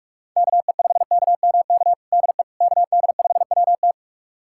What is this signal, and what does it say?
Morse code, heard as a keyed tone, about 41 wpm, 700 Hz. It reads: KE5KMK DE KD5WT
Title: USA Callsign DE Callsign 45wpm